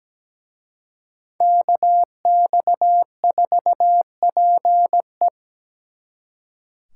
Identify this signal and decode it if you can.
Morse code, heard as a keyed tone, about 20 wpm, 700 Hz. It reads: KX4PE
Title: Speed Racing - USA Call Signs 17wpm